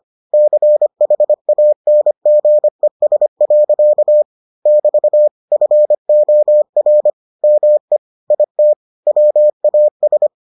Morse code, heard as a keyed tone, about 25 wpm, 600 Hz. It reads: CHANGES. <BT> FOR ME IT WAS